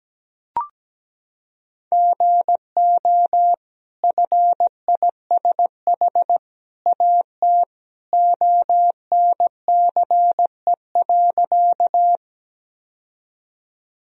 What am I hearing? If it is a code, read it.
Morse code, heard as a keyed tone, about 17 wpm, 700 Hz. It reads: GO FISH AT ONCE.
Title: Sentences from Top 1000 Words - Review Part 1 - 17wpm